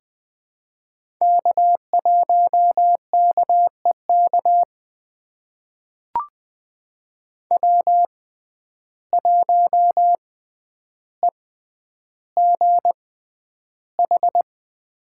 Morse code, heard as a keyed tone, about 20 wpm, spaced as for 10 wpm, 700 Hz. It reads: K1KEK W1EGH
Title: USA Call Signs - 5 Characters - 6x Character Spacing - 20wpm